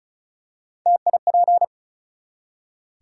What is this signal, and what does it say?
Morse code, heard as a keyed tone, about 36 wpm, 700 Hz. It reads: TIP
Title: Three Letter Words - 35wpm